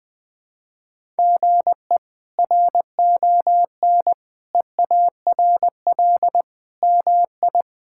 Morse code, heard as a keyed tone, about 20 wpm, 700 Hz. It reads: GE RON EARL MI
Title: K1USN SST QSO Practice - 20wpm